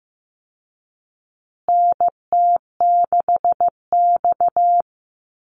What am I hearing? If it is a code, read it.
Morse code, heard as a keyed tone, about 15 wpm, 700 Hz. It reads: NT6X
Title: Call Signs Repeated 2x - 15wpm